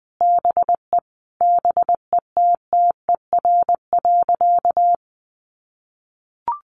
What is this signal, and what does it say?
Morse code, heard as a keyed tone, about 20 wpm, 700 Hz. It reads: BE BETTER.
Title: Sentences from Top 400 Words 20wpm